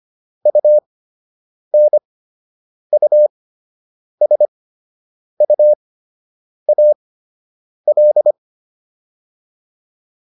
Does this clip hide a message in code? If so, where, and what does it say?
Yes: Morse code from 0.5 to 8.3 s, heard as a keyed tone, about 25 wpm, spaced as for 8 wpm, 600 Hz. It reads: UNUSUAL